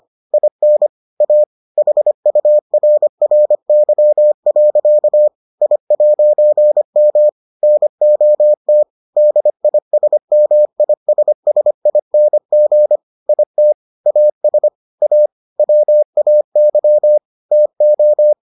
Morse code, heard as a keyed tone, about 25 wpm, 600 Hz. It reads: IN A HURRY. I'M NOT DISMISSING IT AS A WAY TO